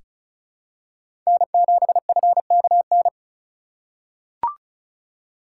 Morse code, heard as a keyed tone, about 36 wpm, 700 Hz. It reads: N7FKN
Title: USA Call Signs - 5 Characters - 2x Character Spacing - 35wpm